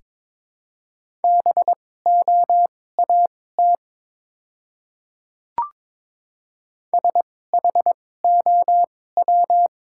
Morse code, heard as a keyed tone, about 22 wpm, spaced as for 16 wpm, 700 Hz. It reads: BOAT SHOW
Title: 2x Character Spacing - Top 500 Words - 22wpm